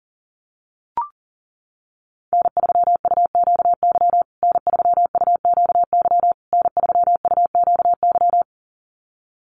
Morse code, heard as a keyed tone, about 37 wpm, 700 Hz. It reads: N3UQY N3UQY N3UQY
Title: USA Call Signs Repeated 3x 40wpm